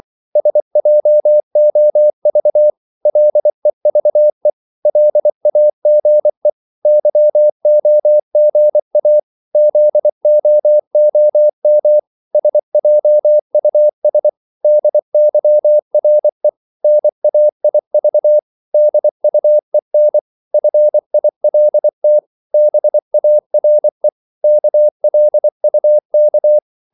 Morse code, heard as a keyed tone, about 24 wpm, 600 Hz. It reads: SJOV LEVE LAGE YOGA ZOOM SJUS DYRE NAIV DUEN FILT BARE KLUK
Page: Korte danske ord 24wpm